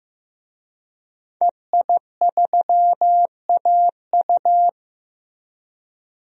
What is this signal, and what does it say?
Morse code, heard as a keyed tone, about 15 wpm, 700 Hz. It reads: EI3AU
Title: DXCC-100 Call Signs - 5 Characters - Repeated 2x - 15wpm